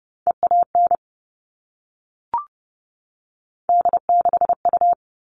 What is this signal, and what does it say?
Morse code, heard as a keyed tone, about 30 wpm, 700 Hz. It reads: EAN D6U
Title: All Cut Numbers - 3-Digit 30wpm